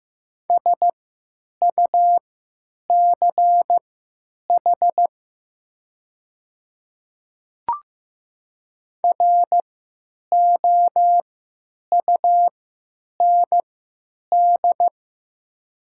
Morse code, heard as a keyed tone, about 15 wpm, spaced as for 9 wpm, 700 Hz. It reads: SUCH ROUND
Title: 3x Character Spacing - Top 300 Words - 15wpm